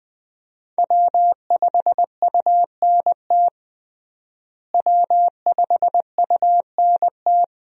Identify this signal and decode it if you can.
Morse code, heard as a keyed tone, about 20 wpm, 700 Hz. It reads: W5UNT W5UNT
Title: USA Call Signs - 5 Characters - Repeated 3x - 20wpm